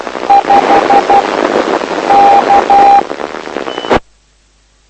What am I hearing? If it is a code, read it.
Morse code, heard as a keyed tone, about 12 wpm, 800 Hz. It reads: 5 K
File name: lu4aao / lu7aa 09dic2011 exp planeador vuelo2